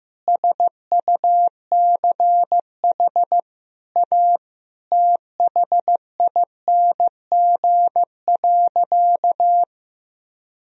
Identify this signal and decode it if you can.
Morse code, heard as a keyed tone, about 15 wpm, 700 Hz. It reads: SUCH A THING.
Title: Sentences from Top 200 Words and Abbreviations 15wpm